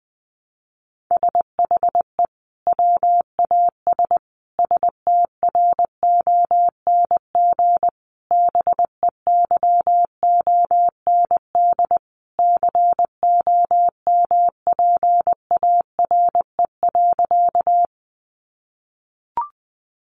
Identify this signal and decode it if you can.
Morse code, heard as a keyed tone, about 20 wpm, 700 Hz. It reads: SHE WAS STRONG BEYOND COMPARE.
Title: Sentences from Top 1300 Words 20wpm